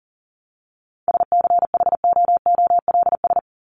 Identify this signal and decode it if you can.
Morse code, heard as a keyed tone, about 40 wpm, 700 Hz. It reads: SCHOOLS
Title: ES:S Suffix - Top 100 Words with Base Words 40wpm